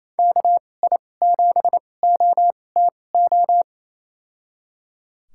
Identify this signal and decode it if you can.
Morse code, heard as a keyed tone, about 28 wpm, spaced as for 20 wpm, 700 Hz. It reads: KI7OTO
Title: USA Call Signs - 6 Characters - 2x Character Spacing - 28wpm